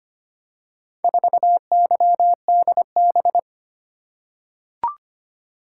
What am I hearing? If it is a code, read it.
Morse code, heard as a keyed tone, about 25 wpm, 700 Hz. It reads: 4YDB
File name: practice / 25wpm / Four Letter-Number Combinations - 25wpm